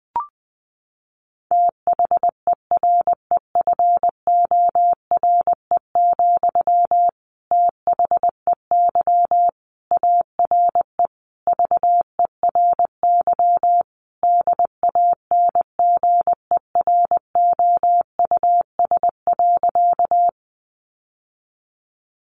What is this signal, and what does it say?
Morse code, heard as a keyed tone, about 20 wpm, 700 Hz. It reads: THEREFORE, THEY ARE VERY DANGEROUS.
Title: Sentences from Top 1200 Words 20wpm